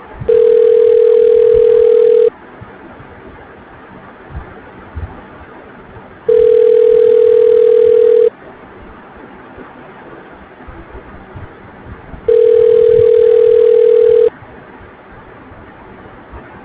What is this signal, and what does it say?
Ringback tone (North America), 440 + 480 Hz.